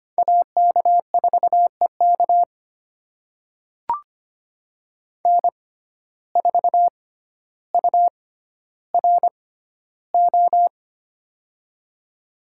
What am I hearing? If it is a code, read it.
Morse code, heard as a keyed tone, about 25 wpm, spaced as for 13 wpm, 700 Hz. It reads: AK4EK N4URO
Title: USA Call Signs - 5 Characters - 6x Character Spacing - 25wpm